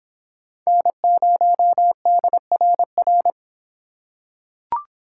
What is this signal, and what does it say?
Morse code, heard as a keyed tone, about 23 wpm, 700 Hz. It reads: N0DRR